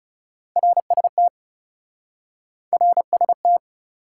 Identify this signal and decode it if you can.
Morse code, heard as a keyed tone, about 38 wpm, 700 Hz. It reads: RST RST
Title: Speed-Racing Top 180 QSO Elements 30wpm